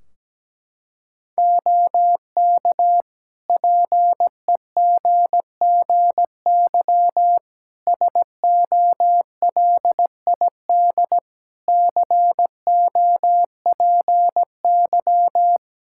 Morse code, heard as a keyed tone, about 17 wpm, 700 Hz. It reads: OK PEGGY SOLID COPY
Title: Practice QSO - Exchange 3 - More Info - 17wpm